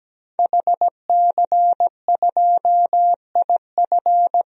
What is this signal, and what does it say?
Morse code, heard as a keyed tone, about 17 wpm, 700 Hz. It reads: HC2IF
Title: DXCC-100 Call Signs - 5 Characters - Repeated 2x - 17wpm